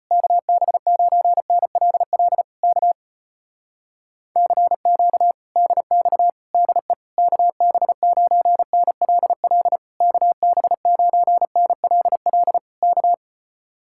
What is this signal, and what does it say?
Morse code, heard as a keyed tone, about 33 wpm, 700 Hz. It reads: KB9NLL K CQ DX DE KB9NLL KB9NLL K